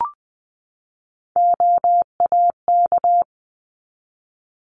Morse code, heard as a keyed tone, about 20 wpm, 700 Hz. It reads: OAK
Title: 260 Lesson 42e - Words 20-20wpm